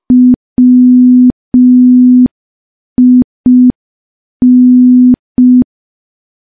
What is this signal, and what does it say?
Morse code, heard as a keyed tone, about 5 wpm, 250 Hz. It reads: WIN
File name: morse / morse-training-05WPM